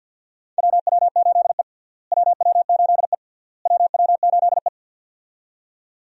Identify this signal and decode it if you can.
Morse code, heard as a keyed tone, about 48 wpm, 700 Hz. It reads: WW8E WW8E WW8E
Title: USA Call Signs - 4 Characters - Repeated 3x - 50wpm